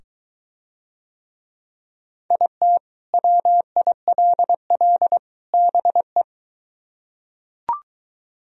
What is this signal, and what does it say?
Morse code, heard as a keyed tone, about 23 wpm, 700 Hz. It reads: IT WILL BE